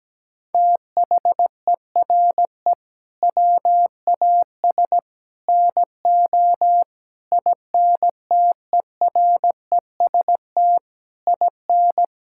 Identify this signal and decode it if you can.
Morse code, heard as a keyed tone, about 17 wpm, 700 Hz. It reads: THERE WAS NO INTEREST IN
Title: Sentences from Top 1000 Words - Review Part 5 - 17wpm